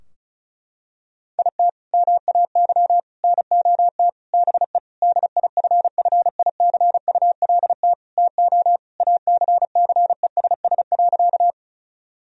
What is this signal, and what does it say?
Morse code, heard as a keyed tone, about 36 wpm, 700 Hz. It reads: IT MAY NOT BE DIFFICULT TO ACCESS.
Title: Sentences from Top 1000 Words - Review Part 4 - 35wpm